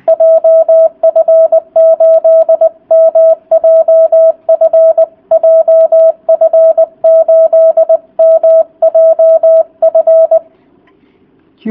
Morse code, heard as a keyed tone, about 20 wpm, 650 Hz. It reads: JF8MJF JF8MJF